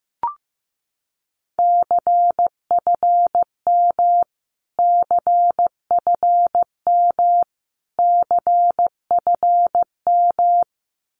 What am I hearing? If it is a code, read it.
Morse code, heard as a keyed tone, about 15 wpm, 700 Hz. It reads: CFM CFM CFM
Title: Abbreviations Repeated 3x - 15wpm